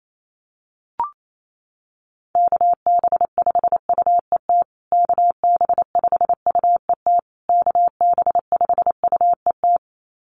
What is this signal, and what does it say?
Morse code, heard as a keyed tone, about 28 wpm, 700 Hz. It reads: KB5UET KB5UET KB5UET
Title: USA Call Signs Repeated 3x 28wpm